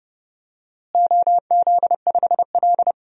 Morse code, heard as a keyed tone, about 31 wpm, 700 Hz. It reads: OZ5L
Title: DXCC-100 Call Signs - 4 Characters - Repeated 2x - 30wpm